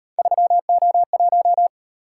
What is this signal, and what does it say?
Morse code, heard as a keyed tone, about 38 wpm, 700 Hz. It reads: 3O1